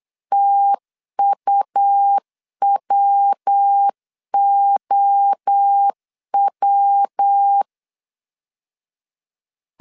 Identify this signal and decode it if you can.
Morse code, heard as a keyed tone, about 8 wpm, 800 Hz. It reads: TUWOW
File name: CW / DET / 1 3 1500C S C 8wpm 4214